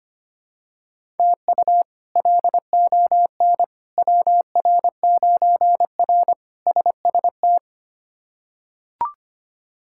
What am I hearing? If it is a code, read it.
Morse code, heard as a keyed tone, about 25 wpm, 700 Hz. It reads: TU LON WR9R SST
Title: K1USN SST QSO Practice - 25wpm